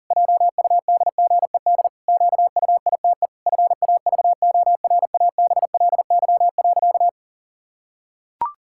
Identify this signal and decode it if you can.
Morse code, heard as a keyed tone, about 40 wpm, 700 Hz. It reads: JUDGED QUITE FAVORABLY.
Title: Pangram Challenge v2 - 40wpm